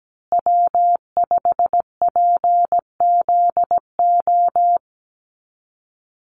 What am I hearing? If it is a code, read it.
Morse code, heard as a keyed tone, about 17 wpm, 700 Hz. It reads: W5PZO
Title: USA Call Signs Repeated 2x 17wpm